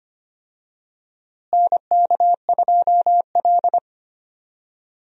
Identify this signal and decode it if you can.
Morse code, heard as a keyed tone, about 25 wpm, 700 Hz. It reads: NK2L
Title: USA Call Signs - 4 Characters - Repeated 3x - 25wpm